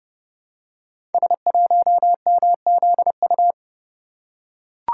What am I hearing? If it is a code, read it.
Morse code, heard as a keyed tone, about 30 wpm, 700 Hz. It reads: S1MZU
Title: Code Groups - 5 Letter:Numbers - 30wpm